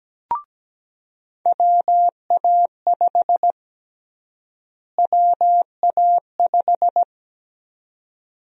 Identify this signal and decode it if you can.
Morse code, heard as a keyed tone, about 17 wpm, 700 Hz. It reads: WA5 WA5
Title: USA Call Sign Prefixes - Repeated 2x - 17wpm